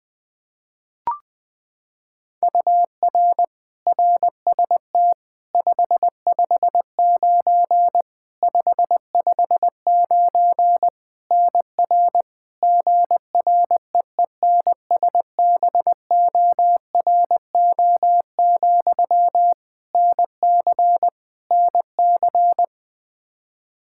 Morse code, heard as a keyed tone, about 20 wpm, 700 Hz. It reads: UR RST 559 559 NR GREENSBORO, NC NC
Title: K1USN SST QSO Practice 20wpm